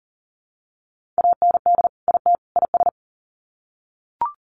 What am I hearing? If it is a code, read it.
Morse code, heard as a keyed tone, about 40 wpm, 700 Hz. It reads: AND IT IS
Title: Sets of 3 Words 40wpm